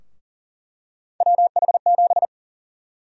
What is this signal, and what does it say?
Morse code, heard as a keyed tone, about 40 wpm, 700 Hz. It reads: WH7
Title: USA Call Sign Prefixes - Repeated 2x - 40wpm